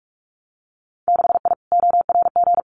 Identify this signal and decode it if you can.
Morse code, heard as a keyed tone, about 44 wpm, 700 Hz. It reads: 6I ORG
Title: Winter Field Day Exchanges Encore 45wpm